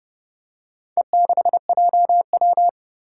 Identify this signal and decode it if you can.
Morse code, heard as a keyed tone, about 30 wpm, 700 Hz. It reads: E6JW
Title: Four Challenging Letter-Number Combinations 30wpm